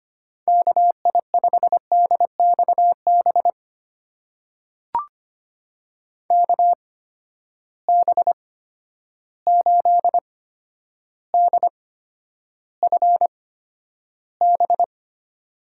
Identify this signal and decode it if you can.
Morse code, heard as a keyed tone, about 25 wpm, spaced as for 11 wpm, 700 Hz. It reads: KI5DXB KB8DFB
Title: USA Call Signs - 6 Characters - 8x Character Spacing - 25wpm